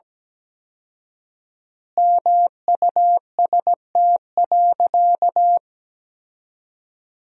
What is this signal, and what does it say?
Morse code, heard as a keyed tone, about 17 wpm, 700 Hz. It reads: MUST.
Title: 4x Word-Spacing - Sentences from Top 200 Words - 17wpm